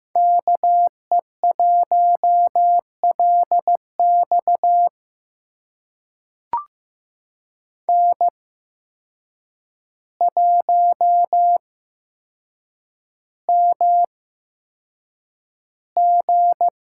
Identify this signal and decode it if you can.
Morse code, heard as a keyed tone, about 15 wpm, spaced as for 7 wpm, 700 Hz. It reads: KE1LX N1MG
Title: USA Call Signs - 5 Characters - 8x Character Spacing - 15wpm